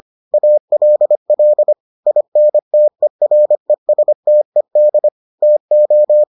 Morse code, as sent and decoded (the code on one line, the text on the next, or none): .- .-.. .-.. / .. -. - . .-. ... - . -.. / - ---
ALL INTERESTED TO